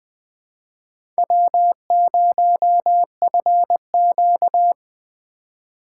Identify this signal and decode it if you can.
Morse code, heard as a keyed tone, about 20 wpm, 700 Hz. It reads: W0FQ
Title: USA Call Signs - 4 Characters - 2x Character Spacing - 20wpm